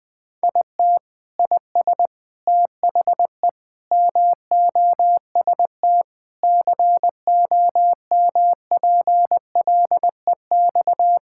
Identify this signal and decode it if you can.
Morse code, heard as a keyed tone, about 21 wpm, 700 Hz. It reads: IT IS THE MOST COMPLEX
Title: Sentences from Top 1000 Words - Review Part 6 - 20wpm